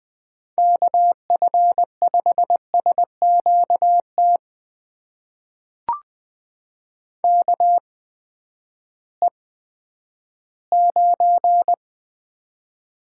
Morse code, heard as a keyed tone, about 20 wpm, spaced as for 11 wpm, 700 Hz. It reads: KF5SQT KE9
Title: USA Call Signs - 8x Character Spacing - 20wpm